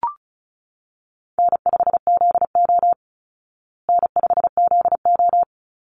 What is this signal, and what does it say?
Morse code, heard as a keyed tone, about 34 wpm, 700 Hz. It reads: N5ZO N5ZO
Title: Easy Call Signs Repeated 2x - 35wpm